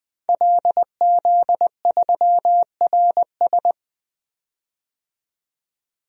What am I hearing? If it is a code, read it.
Morse code, heard as a keyed tone, about 20 wpm, 700 Hz. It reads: LZ3RS
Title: DXCC-100 Call Signs - 5 Characters - 20wpm